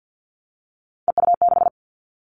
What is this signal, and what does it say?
Morse code, heard as a keyed tone, about 50 wpm, 700 Hz. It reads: EU6
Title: Callsign Prefixes for Top 100 DXCC - 50wpm